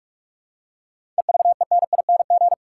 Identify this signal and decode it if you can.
Morse code, heard as a keyed tone, about 46 wpm, 700 Hz. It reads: EVENING
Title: Seven Letter Words - 45wpm